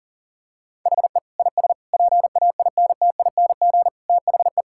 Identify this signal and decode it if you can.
Morse code, heard as a keyed tone, about 37 wpm, 700 Hz. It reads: HE IS PAINTING THE
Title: Sentences from Top 1000 Words - Review Part 6 - 40wpm